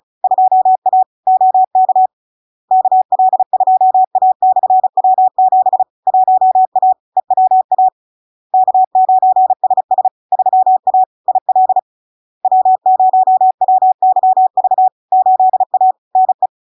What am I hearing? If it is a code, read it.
Morse code, heard as a keyed tone, about 36 wpm, 750 Hz. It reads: 2A OK KL2A/W7 1A EWA K9SS 3A IL W0WYV 8A NE
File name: LARK-FD-2022-Log-in-CW-35wpm-750-Hz